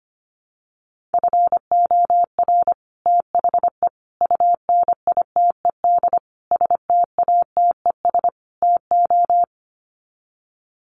Morse code, heard as a keyed tone, about 25 wpm, 700 Hz. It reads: FOR THE UNITED STATES TO
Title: Sets of 5 Words Encore 25wpm